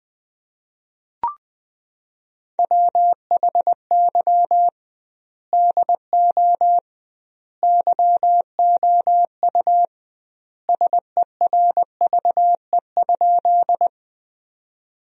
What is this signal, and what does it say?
Morse code, heard as a keyed tone, about 20 wpm, 700 Hz. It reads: WHY DO YOU SERVE?
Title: 2x Word-Spacing - Sentences from Top 400 Words - 20wpm